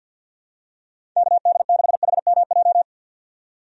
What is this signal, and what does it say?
Morse code, heard as a keyed tone, about 50 wpm, 700 Hz. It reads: KD6HMJ
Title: USA Call Signs - 6 Characters - 50wpm